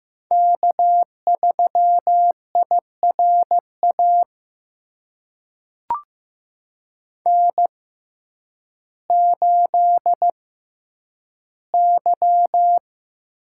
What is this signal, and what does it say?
Morse code, heard as a keyed tone, about 15 wpm, spaced as for 9 wpm, 700 Hz. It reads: K3IRA N8Y
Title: USA Call Signs - 5 Characters - 6x Character Spacing - 15wpm